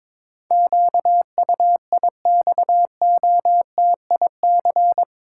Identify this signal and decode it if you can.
Morse code, heard as a keyed tone, about 21 wpm, 700 Hz. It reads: QUIXOTIC
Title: Pangram Challenge v2 - 22wpm